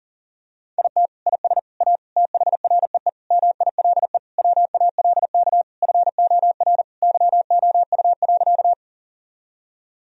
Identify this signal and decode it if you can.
Morse code, heard as a keyed tone, about 40 wpm, 700 Hz. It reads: IT IS A THREE MILE WALK FOR YOU.
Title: Sentences from Top 1000 Words - Review Part 1 - 40wpm